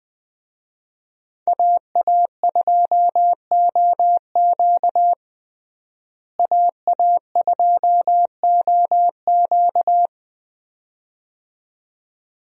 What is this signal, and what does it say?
Morse code, heard as a keyed tone, about 20 wpm, 700 Hz. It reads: AA2OQ AA2OQ
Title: USA Call Signs - 5 Characters - Repeated 2x - 20wpm